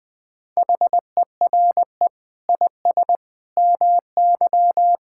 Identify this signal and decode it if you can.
Morse code, heard as a keyed tone, about 21 wpm, 700 Hz. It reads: HERE IS MY
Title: Sentences from Top 1000 Words - Review Part 5 - 20wpm